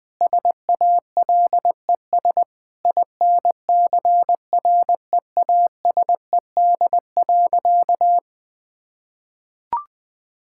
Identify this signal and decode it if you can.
Morse code, heard as a keyed tone, about 20 wpm, 700 Hz. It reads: SALES INCREASED.
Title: Sentences with 7-Word Phrases - 20wpm